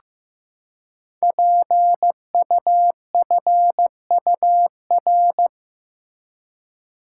Morse code, heard as a keyed tone, about 15 wpm, 700 Hz. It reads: PUFUR
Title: Code Groups - 5 Letters - 15wpm